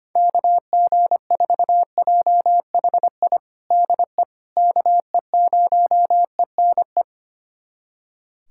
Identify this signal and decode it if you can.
Morse code, heard as a keyed tone, about 25 wpm, 700 Hz. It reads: KG4JHI DE KE0ENE
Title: USA Callsign DE Callsign 25wpm